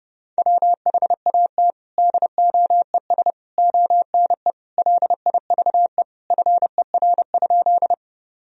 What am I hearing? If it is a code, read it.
Morse code, heard as a keyed tone, about 30 wpm, 700 Hz. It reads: WHAT DOES ONE LIVE FER?
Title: Sentences from Top 200 Words and Abbreviations 30wpm